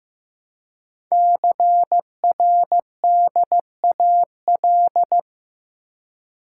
Morse code, heard as a keyed tone, about 15 wpm, 700 Hz. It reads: CRDAL
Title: Code Groups - 5 Letters - 15wpm